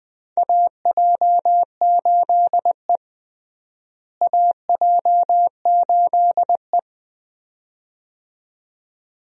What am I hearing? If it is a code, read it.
Morse code, heard as a keyed tone, about 20 wpm, 700 Hz. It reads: AJ8E AJ8E
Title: USA Call Signs - 4 Characters - Repeated 2x - 20wpm